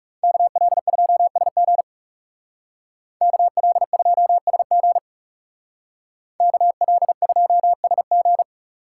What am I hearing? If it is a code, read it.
Morse code, heard as a keyed tone, about 39 wpm, 700 Hz. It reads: KL2SG KL2SG KL2SG